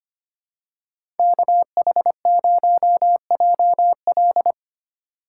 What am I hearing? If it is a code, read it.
Morse code, heard as a keyed tone, about 25 wpm, 700 Hz. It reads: KH0JL
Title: USA Call Signs - 5 Characters - 4x Character Spacing - 25wpm